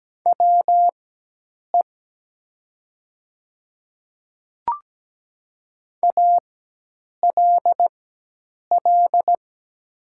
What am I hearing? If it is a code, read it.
Morse code, heard as a keyed tone, about 17 wpm, spaced as for 8 wpm, 700 Hz. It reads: WE ALL
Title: 4x Character Spacing - Top 300 Words - 17wpm